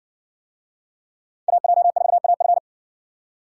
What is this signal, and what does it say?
Morse code, heard as a keyed tone, about 60 wpm, 700 Hz. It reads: AJ4AF